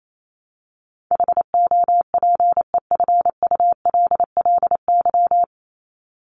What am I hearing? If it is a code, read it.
Morse code, heard as a keyed tone, about 28 wpm, 700 Hz. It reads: HOPEFULLY
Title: Top 1251-1500 Words 28wpm